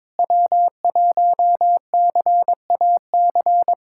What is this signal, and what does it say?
Morse code, heard as a keyed tone, about 22 wpm, 700 Hz. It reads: W1CAC
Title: USA Callsign DE Callsign - 5 Characters - 22wpm